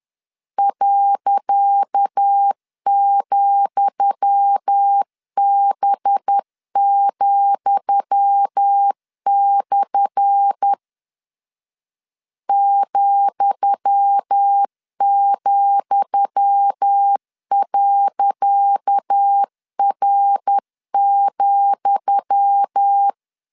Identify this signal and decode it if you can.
Morse code, heard as a keyed tone, about 10 wpm, 800 Hz. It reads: .,B,/ ,,.R,
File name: LEZIONI / 10 SIGNS / 10D COMA 10wpm 3013